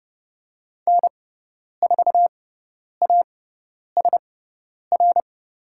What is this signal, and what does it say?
Morse code, heard as a keyed tone, about 30 wpm, spaced as for 10 wpm, 700 Hz. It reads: N4ASR
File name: Lesson 18g - Letter m - Call Signs 30wpm